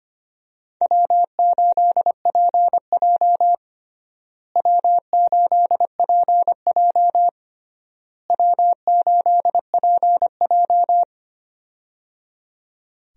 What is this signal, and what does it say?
Morse code, heard as a keyed tone, about 25 wpm, 700 Hz. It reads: W8PJ W8PJ W8PJ
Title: USA Call Signs - 4 Characters - Repeated 3x - 25wpm